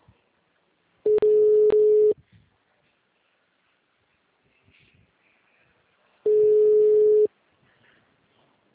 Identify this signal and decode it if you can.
Ringback tone (Europe, 425 Hz).
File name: out- -2034